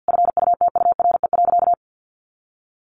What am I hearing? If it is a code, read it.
Morse code, heard as a keyed tone, about 50 wpm, 700 Hz. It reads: FUTURE.
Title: Sentences from Top 1100 Words 50wpm